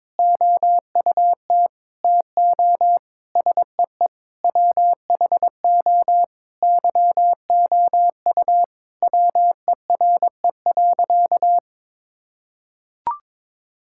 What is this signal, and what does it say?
Morse code, heard as a keyed tone, about 22 wpm, 700 Hz. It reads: OUT TO SEE WHO YOU WERE.
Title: Sentences from Top 1000 Words - Review Part 6 - 22wpm